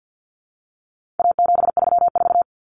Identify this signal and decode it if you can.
Morse code, heard as a keyed tone, about 50 wpm, 700 Hz. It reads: A734
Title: Common Cut Numbers - 4-Digit 50wpm